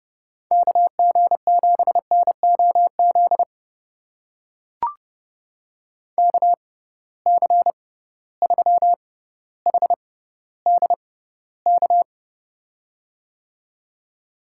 Morse code, heard as a keyed tone, about 30 wpm, spaced as for 15 wpm, 700 Hz. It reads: KG7NOZ KC3HDK
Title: USA Call Signs - 6 Characters - 6x Character Spacing - 30wpm